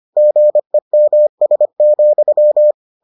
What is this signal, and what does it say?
Morse code, heard as a keyed tone, about 25 wpm, 600 Hz. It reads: GEMS,